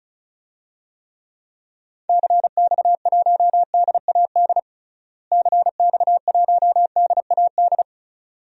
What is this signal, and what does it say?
Morse code, heard as a keyed tone, about 36 wpm, 700 Hz. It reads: CX1DAD CX1DAD
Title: DXCC-100 Call Signs - 6 Characters - Repeated 2x - 35wpm